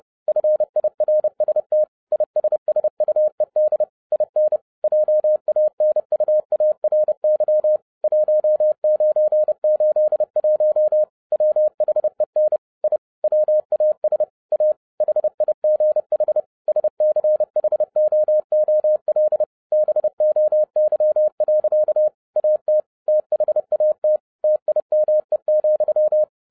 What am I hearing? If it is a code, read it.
Morse code, heard as a keyed tone, about 30 wpm, 610 Hz. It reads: FIRST ISSUED IN JANUARY 1981 WHEN I WAS A HIGH SCHOOL BOY. AT THAT TIME,
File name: 2011-07-mp3 / n40 EXCITING QRP CW OPERATION 30wpm